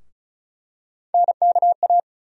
Morse code, heard as a keyed tone, about 36 wpm, 700 Hz. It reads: NKA
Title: USA Call Sign Postfixes - 3 Letters 35wpm